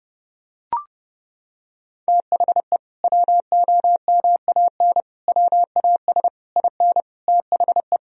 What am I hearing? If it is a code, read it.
Morse code, heard as a keyed tone, about 30 wpm, 700 Hz. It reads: THE WOMAN WAS IN THE
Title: Sentences from Top 1000 Words - Review Part 3 - 30wpm